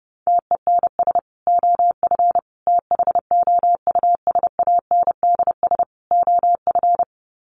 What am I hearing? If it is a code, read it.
Morse code, heard as a keyed tone, about 30 wpm, 700 Hz. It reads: TENS OF THOUSANDS OF
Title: Sets of 4 Words 30wpm